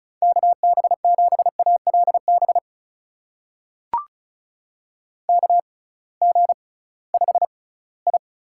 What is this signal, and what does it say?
Morse code, heard as a keyed tone, about 36 wpm, spaced as for 21 wpm, 700 Hz. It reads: KB7ALB KG5I
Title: USA Call Signs - 6x Character Spacing - 35wpm